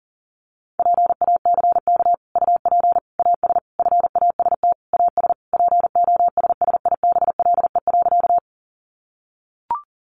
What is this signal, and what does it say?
Morse code, heard as a keyed tone, about 40 wpm, 700 Hz. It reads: PACK UP AS FAST AS POSSIBLE.
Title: Sentences from Top 1300 Words 40wpm